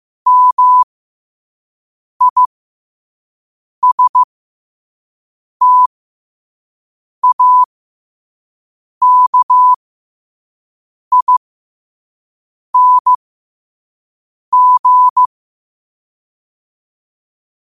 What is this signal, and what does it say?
Morse code, heard as a keyed tone, about 15 wpm, spaced as for 5 wpm, 1000 Hz. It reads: MISTAKING
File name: qotdfiles / 5wpm